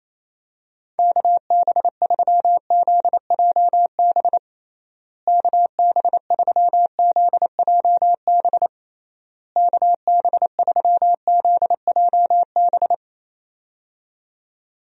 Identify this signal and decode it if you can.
Morse code, heard as a keyed tone, about 28 wpm, 700 Hz. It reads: KB3ZJB KB3ZJB KB3ZJB
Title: USA Call Signs - 6 Characters - Repeated 3x - 28wpm